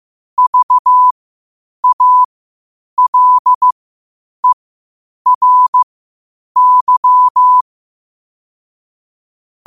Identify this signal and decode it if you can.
Morse code, heard as a keyed tone, about 15 wpm, spaced as for 8 wpm, 1000 Hz. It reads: VALERY
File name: qotdfiles / 8wpm